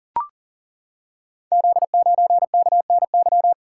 Morse code, heard as a keyed tone, about 40 wpm, 700 Hz. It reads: Z9KNY
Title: Code Groups - 5 Letter:Numbers - 40wpm